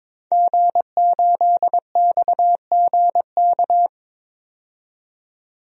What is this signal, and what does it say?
Morse code, heard as a keyed tone, about 22 wpm, 700 Hz. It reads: G8XGK
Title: DXCC-100 Call Signs - 5 Characters - 22wpm